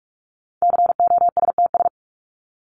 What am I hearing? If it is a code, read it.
Morse code, heard as a keyed tone, about 44 wpm, 700 Hz. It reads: COSTS
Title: ES:S Suffix - Top 100 Words with Base Words 45wpm